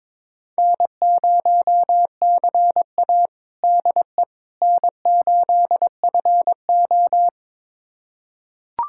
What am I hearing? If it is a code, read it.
Morse code, heard as a keyed tone, about 22 wpm, 700 Hz. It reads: N0CA DE N8FO
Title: USA Callsign DE Callsign - 4 Characters - 22wpm